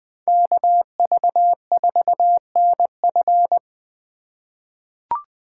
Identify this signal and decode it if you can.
Morse code, heard as a keyed tone, about 20 wpm, 700 Hz. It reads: KV4NF
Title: USA Call Signs - 3x Character Spacing - 20wpm